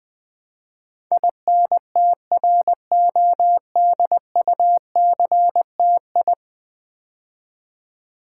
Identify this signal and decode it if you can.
Morse code, heard as a keyed tone, about 20 wpm, 700 Hz. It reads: INTRODUCTI
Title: Build A Word - 5 Letters At A Time - 12-Letter Words - 20wpm